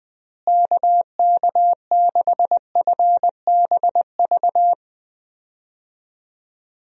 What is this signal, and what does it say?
Morse code, heard as a keyed tone, about 20 wpm, 700 Hz. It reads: KK6FBV
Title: USA Call Signs - 6 Characters - Repeated 3x - 20wpm